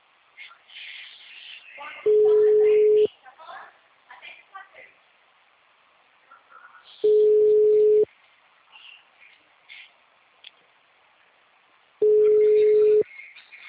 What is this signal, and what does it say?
Ringback tone (Europe, 425 Hz), with speech.
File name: out- -2011- - - .52